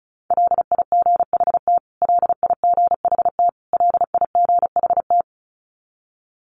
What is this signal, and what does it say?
Morse code, heard as a keyed tone, about 34 wpm, 700 Hz. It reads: LIGHT LIGHT LIGHT
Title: Top 200 Words Repeated 35wpm